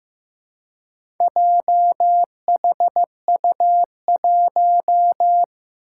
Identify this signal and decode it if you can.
Morse code, heard as a keyed tone, about 15 wpm, 700 Hz. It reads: JHU1
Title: Four Challenging Letter-Number Combinations 15wpm